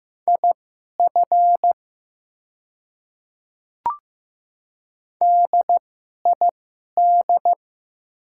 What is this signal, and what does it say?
Morse code, heard as a keyed tone, about 15 wpm, spaced as for 11 wpm, 700 Hz. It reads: IF DID
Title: 2x Character Spacing - Top 200 Words - 15wpm